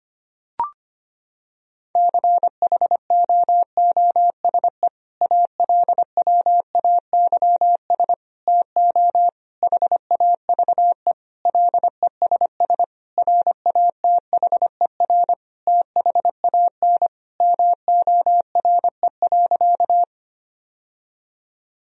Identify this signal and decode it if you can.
Morse code, heard as a keyed tone, about 25 wpm, 700 Hz. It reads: CHOOSE ALWAYS TO HAVE LESS RATHER THAN MORE.
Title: Sentences from Top 1000 Words - Review Part 4 - 25wpm